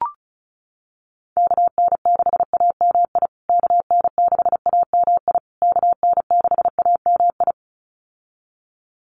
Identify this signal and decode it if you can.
Morse code, heard as a keyed tone, about 34 wpm, 700 Hz. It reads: KN6AMI KN6AMI KN6AMI
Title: USA Call Signs Repeated 3x 35wpm